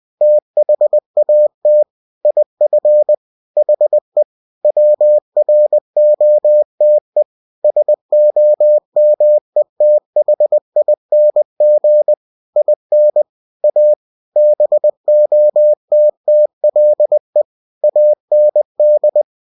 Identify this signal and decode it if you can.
Morse code, heard as a keyed tone, about 20 wpm, 600 Hz. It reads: THAT IF HE WROTE SOMETHING IN A BOTTLE AND
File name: cw-books / winnie-the-pooh / winnie-20wpm-page-0092